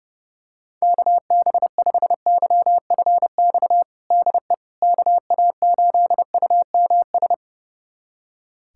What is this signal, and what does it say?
Morse code, heard as a keyed tone, about 29 wpm, 700 Hz. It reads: KB5YFX DE KA8UMS
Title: USA Callsign DE Callsign - 6 Characters - 30wpm